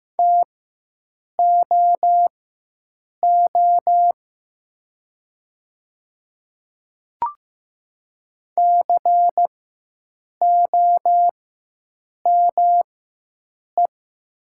Morse code, heard as a keyed tone, about 15 wpm, spaced as for 7 wpm, 700 Hz. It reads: TOO COME
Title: 4x Character Spacing - Top 300 Words - 15wpm